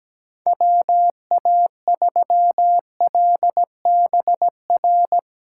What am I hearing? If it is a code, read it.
Morse code, heard as a keyed tone, about 17 wpm, 700 Hz. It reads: WA3LBR
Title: USA Callsign DE Callsign - 6 Characters - 17wpm